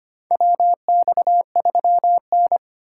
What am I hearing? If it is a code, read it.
Morse code, heard as a keyed tone, about 25 wpm, 700 Hz. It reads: WX3N
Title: USA Call Signs - 4 Characters - 2x Character Spacing - 25wpm